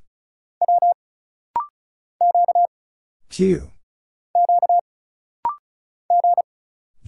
Morse code, heard as a keyed tone, about 36 wpm, 700 Hz. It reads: W Q Q G
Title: Single Letters - Rapid-Fire 35wpm